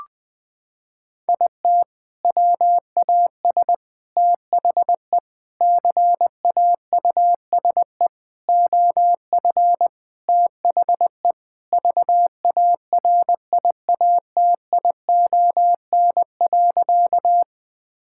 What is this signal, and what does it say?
Morse code, heard as a keyed tone, about 21 wpm, 700 Hz. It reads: IT WAS THE CAUSE OF THE VARIATION.
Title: Sentences from Top 1600 Words 20wpm